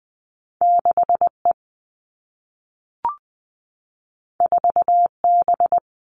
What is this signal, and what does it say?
Morse code, heard as a keyed tone, about 20 wpm, 700 Hz. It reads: 6 E 4 B
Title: All Cut Numbers - 2-Digit 20wpm